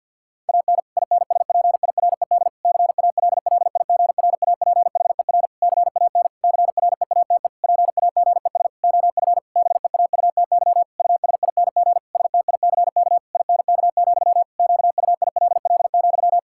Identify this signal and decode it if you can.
Morse code, heard as a keyed tone, about 50 wpm, 700 Hz. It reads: AN INSPIRED CALLIGRAPHER CAN CREATE PAGES OF BEAUTY USING STICK INK, QUILL,
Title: Pangram Challenge v2 - 50wpm